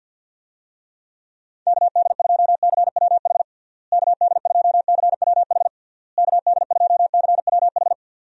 Morse code, heard as a keyed tone, about 49 wpm, 700 Hz. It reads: KD2CWH KD2CWH KD2CWH
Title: USA Call Signs - 6 Characters - Repeated 3x - 50wpm